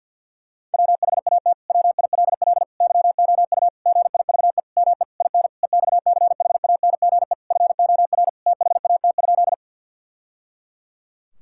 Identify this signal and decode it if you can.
Morse code, heard as a keyed tone, about 50 wpm, 700 Hz. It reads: WHAT WILL YOU GIVE ME IN EXCHANGE FOR THAT?
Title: Sentences from Top 1000 Words - Review Part 6 - 50wpm